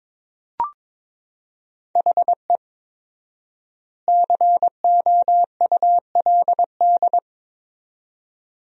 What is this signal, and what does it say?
Morse code, heard as a keyed tone, about 22 wpm, 700 Hz. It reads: HE COULD
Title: 4x Word-Spacing - Sentences from Top 300 Words - 22wpm